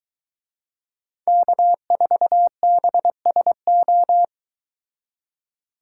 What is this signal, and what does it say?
Morse code, heard as a keyed tone, about 22 wpm, 700 Hz. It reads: K4BSO